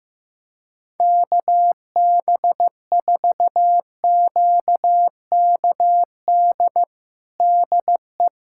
Morse code, heard as a keyed tone, about 15 wpm, 700 Hz. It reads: KB4QKD DE
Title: USA Callsign DE Callsign - 6 Characters - 15wpm